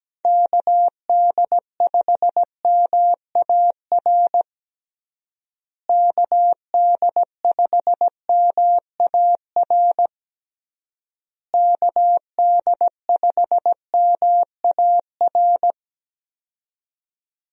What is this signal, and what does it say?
Morse code, heard as a keyed tone, about 17 wpm, 700 Hz. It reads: KD5MAR KD5MAR KD5MAR
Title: USA Call Signs - 6 Characters - Repeated 3x - 17wpm